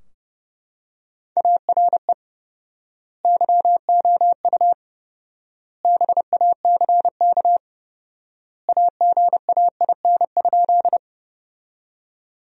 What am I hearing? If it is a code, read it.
Morse code, heard as a keyed tone, about 32 wpm, 700 Hz. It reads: ARE YOU BACK AGAIN?
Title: 4x Word-Spacing - Sentences from Top 200 Words - 30wpm